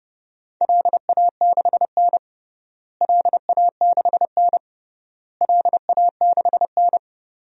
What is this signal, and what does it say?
Morse code, heard as a keyed tone, about 32 wpm, 700 Hz. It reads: LA6N LA6N LA6N
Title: DXCC-100 Call Signs - 4 Characters - Repeated 3x - 30wpm